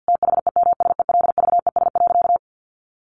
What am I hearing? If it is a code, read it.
Morse code, heard as a keyed tone, about 50 wpm, 700 Hz. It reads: THEMSELVES.
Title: Sentences from Top 1300 Words 50wpm